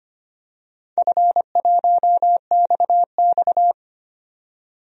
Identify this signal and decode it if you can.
Morse code, heard as a keyed tone, about 25 wpm, 700 Hz. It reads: F1XX
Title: DXCC-100 Call Signs - 4 Characters - 25wpm